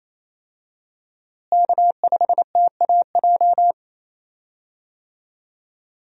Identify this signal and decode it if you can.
Morse code, heard as a keyed tone, about 28 wpm, 700 Hz. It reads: K5TAJ
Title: USA Call Signs - 5 Characters - Repeated 3x - 28wpm